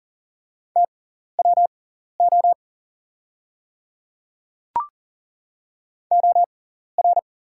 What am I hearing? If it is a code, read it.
Morse code, heard as a keyed tone, about 40 wpm, spaced as for 14 wpm, 690 Hz. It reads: TWO OR
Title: 6x Character Spacing - Top 100 Words - 40wpm